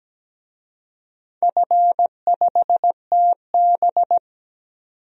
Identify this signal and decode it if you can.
Morse code, heard as a keyed tone, about 17 wpm, 700 Hz. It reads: F5TB
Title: DXCC-100 Call Signs - 4 Characters - 17wpm